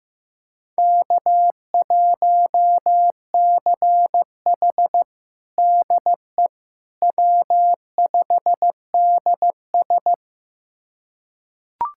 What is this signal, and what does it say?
Morse code, heard as a keyed tone, about 16 wpm, 700 Hz. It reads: K1CH DE W5DS
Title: USA Callsign DE Callsign - 4 Characters - 15wpm